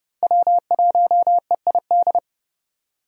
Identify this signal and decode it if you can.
Morse code, heard as a keyed tone, about 32 wpm, 700 Hz. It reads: W1EID
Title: USA Call Signs - 5 Characters - 30wpm